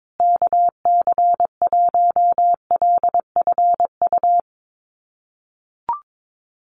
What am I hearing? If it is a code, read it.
Morse code, heard as a keyed tone, about 22 wpm, 700 Hz. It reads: KC1LFU
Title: USA Call Signs Repeated 2x 22wpm